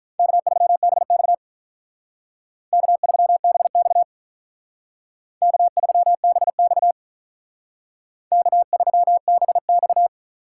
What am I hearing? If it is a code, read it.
Morse code, heard as a keyed tone, about 45 wpm, 700 Hz. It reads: K3BX K3BX K3BX K3BX